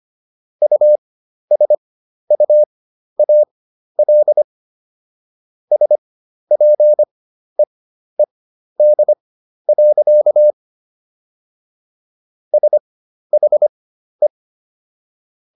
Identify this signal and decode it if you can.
Morse code, heard as a keyed tone, about 25 wpm, spaced as for 12 wpm, 600 Hz. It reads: USUAL SPEED. SHE